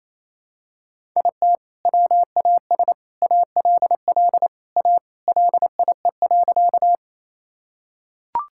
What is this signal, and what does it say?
Morse code, heard as a keyed tone, about 27 wpm, 700 Hz. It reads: IT WAS ALL A LIE.
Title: Sentences from Top 1600 Words 28wpm